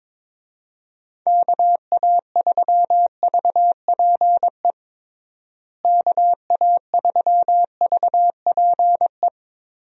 Morse code, heard as a keyed tone, about 22 wpm, 700 Hz. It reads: KA3VPE KA3VPE
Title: USA Call Signs - 6 Characters - Repeated 3x - 22wpm